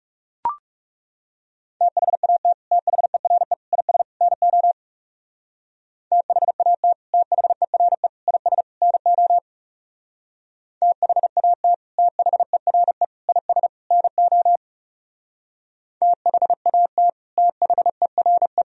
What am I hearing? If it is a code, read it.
Morse code, heard as a keyed tone, about 35 wpm, 700 Hz. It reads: THAT THERE IS NO THAT THERE IS NO THAT THERE IS NO THAT THERE
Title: Speed-Racing - Sets of 4 Words 30wpm